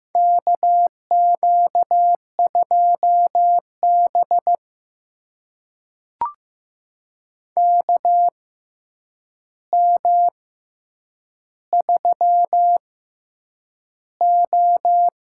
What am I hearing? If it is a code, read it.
Morse code, heard as a keyed tone, about 15 wpm, spaced as for 8 wpm, 700 Hz. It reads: KQ2B KM3O
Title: USA Call Signs - 4 Characters - 6x Character Spacing - 15wpm